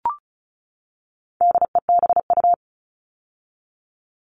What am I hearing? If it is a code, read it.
Morse code, heard as a keyed tone, about 34 wpm, 700 Hz. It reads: DEBU
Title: All Cut Numbers - 4-Digit 35wpm